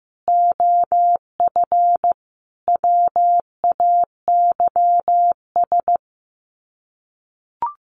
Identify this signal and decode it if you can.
Morse code, heard as a keyed tone, about 15 wpm, 700 Hz. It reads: OF WAYS
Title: Sets of 5 Words Encore 15wpm